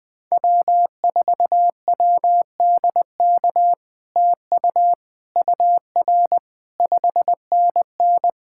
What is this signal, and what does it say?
Morse code, heard as a keyed tone, about 20 wpm, 700 Hz. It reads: W4WDK TU UR 5NN
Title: POTA QSO Practice - 20wpm